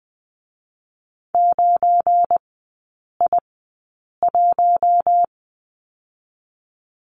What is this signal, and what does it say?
Morse code, heard as a keyed tone, about 20 wpm, spaced as for 8 wpm, 690 Hz. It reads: 9I1
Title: 172 Lesson 28e - Number 9 - Three Characters 20wpm